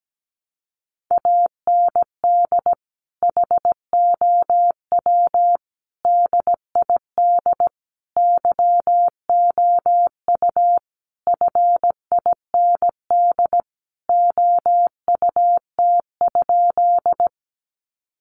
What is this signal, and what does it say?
Morse code, heard as a keyed tone, about 17 wpm, 700 Hz. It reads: AND HOW DID YOU FIND OUT?
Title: Sentences from Top 200 Words 17wpm